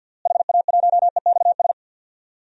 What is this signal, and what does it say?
Morse code, heard as a keyed tone, about 47 wpm, 700 Hz. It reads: HA1EXS
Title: DXCC-100 Call Signs - 6 Characters - Repeated 3x - 50wpm